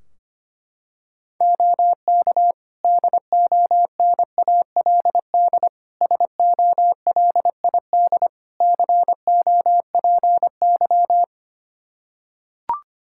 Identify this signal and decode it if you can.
Morse code, heard as a keyed tone, about 25 wpm, 700 Hz. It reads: OK DONALD SOLID COPY
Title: Practice QSO - Exchange 3 - More Info - 25wpm